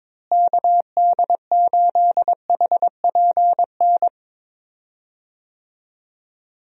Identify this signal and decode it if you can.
Morse code, heard as a keyed tone, about 22 wpm, 700 Hz. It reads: KD8HPN
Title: USA Call Signs - 6 Characters - Repeated 3x - 22wpm